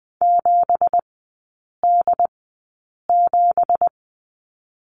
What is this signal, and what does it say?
Morse code, heard as a keyed tone, about 20 wpm, spaced as for 8 wpm, 690 Hz. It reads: 7D7
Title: 166 Lesson 27e - Number 7 - Three Characters 20wpm